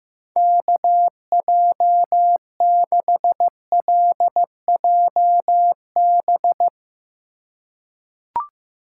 Morse code, heard as a keyed tone, about 15 wpm, 700 Hz. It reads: KJ6LJB
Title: USA Call Signs - 6 Characters - 4x Character Spacing - 15wpm